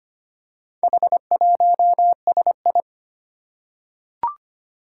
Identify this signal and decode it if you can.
Morse code, heard as a keyed tone, about 25 wpm, 700 Hz. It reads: H1SI
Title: Four Challenging Letter-Number Combinations 25wpm